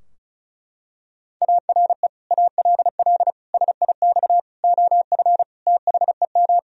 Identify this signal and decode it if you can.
Morse code, heard as a keyed tone, about 36 wpm, 700 Hz. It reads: ARE ALL SIX OF THEM
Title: Sentences from Top 1000 Words - Review Part 2 - 35wpm